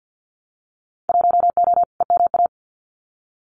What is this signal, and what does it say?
Morse code, heard as a keyed tone, about 50 wpm, 700 Hz. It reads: 1O EMA
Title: Winter Field Day Exchanges Encore 50wpm